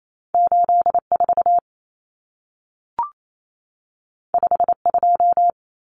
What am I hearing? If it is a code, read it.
Morse code, heard as a keyed tone, about 28 wpm, 700 Hz. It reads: 84 52
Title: Numbers - 2-Digit 28wpm